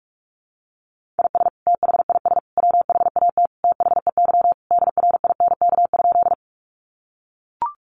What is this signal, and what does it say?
Morse code, heard as a keyed tone, about 44 wpm, 700 Hz. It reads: IS THIS WHAT THEY DRINK?
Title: Sentences from Top 1500 Words 45wpm